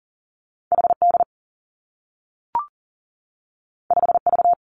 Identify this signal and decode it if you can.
Morse code, heard as a keyed tone, about 40 wpm, 700 Hz. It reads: HD 5V
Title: Two Challenging Letter-Number Combinations 40wpm